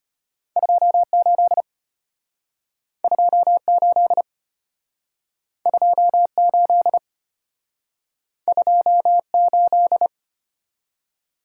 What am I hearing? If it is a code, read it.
Morse code, heard as a keyed tone, about 34 wpm, 700 Hz. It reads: 28 28 28 28